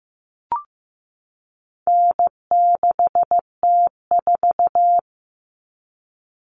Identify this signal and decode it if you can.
Morse code, heard as a keyed tone, about 15 wpm, 700 Hz. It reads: N6T4
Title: All Cut Numbers - 4-Digit 15wpm